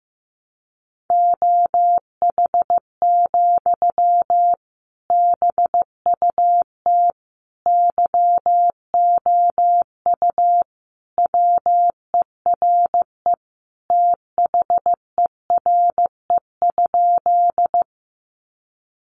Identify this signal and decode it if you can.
Morse code, heard as a keyed tone, about 15 wpm, 700 Hz. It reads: OH, BUT YOU WERE THERE?
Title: Sentences from Top 500 Words 15wpm